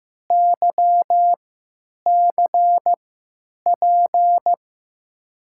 Morse code, heard as a keyed tone, about 16 wpm, spaced as for 9 wpm, 700 Hz. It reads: YCP
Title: USA Call Signs - 6 Characters - 3x Character Spacing - 15wpm